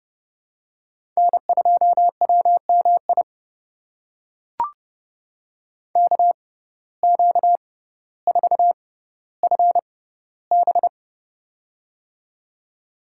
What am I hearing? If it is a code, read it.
Morse code, heard as a keyed tone, about 31 wpm, spaced as for 15 wpm, 700 Hz. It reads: N2WMI KQ4FB
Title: USA Call Signs - 5 Characters - 6x Character Spacing - 30wpm